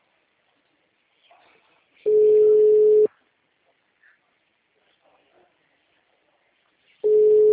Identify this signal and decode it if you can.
Ringback tone (Europe, 425 Hz).
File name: out- -2009- - - .56